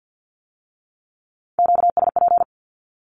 Morse code, heard as a keyed tone, about 50 wpm, 700 Hz. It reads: QSP
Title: Q-Codes with Spoken Meaning - 50wpm